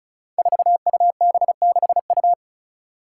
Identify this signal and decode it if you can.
Morse code, heard as a keyed tone, about 34 wpm, 700 Hz. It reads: 4UB6U
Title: Five Challenging Letter-Number Combinations 35wpm